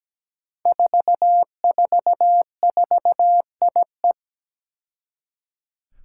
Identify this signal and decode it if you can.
Morse code, heard as a keyed tone, about 17 wpm, 700 Hz. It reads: 444IE
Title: Five Challenging Letter-Number Combinations 17wpm